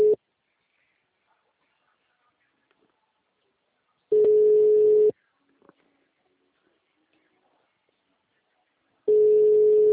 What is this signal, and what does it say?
Ringback tone (Europe, 425 Hz).
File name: out- -2014- - - .110